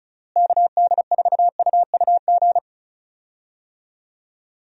Morse code, heard as a keyed tone, about 36 wpm, 700 Hz. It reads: KD4UUG
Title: USA Call Signs - 6 Characters - Repeated 2x - 35wpm